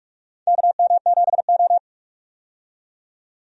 Morse code, heard as a keyed tone, about 46 wpm, 700 Hz. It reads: KM7O
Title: USA Call Signs - 4 Characters - 45wpm